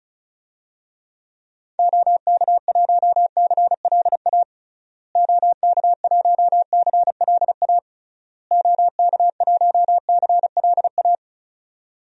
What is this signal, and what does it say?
Morse code, heard as a keyed tone, about 36 wpm, 700 Hz. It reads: OK1CLA OK1CLA OK1CLA
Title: DXCC-100 Call Signs - 6 Characters - Repeated 3x - 35wpm